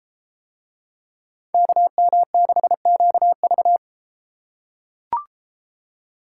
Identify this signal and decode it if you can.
Morse code, heard as a keyed tone, about 31 wpm, 700 Hz. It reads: KM6QV